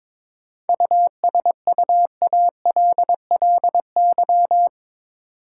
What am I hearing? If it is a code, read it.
Morse code, heard as a keyed tone, about 22 wpm, 700 Hz. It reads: USUALLY
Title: Sentences from Top 1000 Words - Review Part 3 - New Word Drill - 22wpm